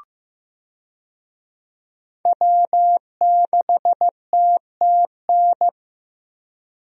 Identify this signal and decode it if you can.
Morse code, heard as a keyed tone, about 15 wpm, 700 Hz. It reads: W6TTN
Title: USA Call Signs - 5 Characters - Repeated 2x - 15wpm